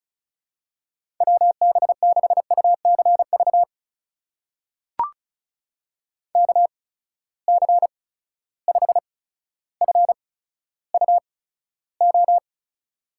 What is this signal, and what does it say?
Morse code, heard as a keyed tone, about 36 wpm, spaced as for 15 wpm, 700 Hz. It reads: WB6UCV KC5FUO
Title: USA Call Signs - 8x Character Spacing - 35wpm